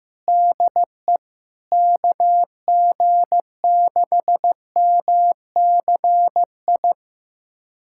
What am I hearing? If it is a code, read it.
Morse code, heard as a keyed tone, about 14 wpm, 700 Hz. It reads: DE KG6MCI
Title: USA Callsign DE Callsign 15wpm